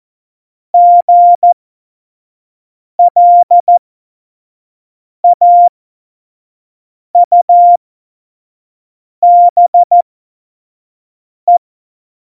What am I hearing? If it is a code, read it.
Morse code, heard as a keyed tone, about 14 wpm, spaced as for 5 wpm, 700 Hz. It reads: GLAUBE
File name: Morsekurs / Witze / Witz 28. August 2018, 25 F70